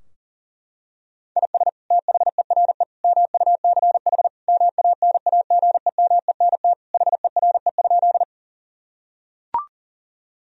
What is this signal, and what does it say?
Morse code, heard as a keyed tone, about 40 wpm, 700 Hz. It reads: IS THERE MUCH MANAGEMENT HERE?
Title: Sentences from Top 1000 Words - Review Part 4 - 40wpm